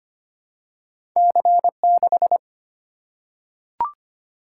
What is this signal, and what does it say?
Morse code, heard as a keyed tone, about 25 wpm, 700 Hz. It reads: C6
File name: Two Letter-Number Combinations - 25wpm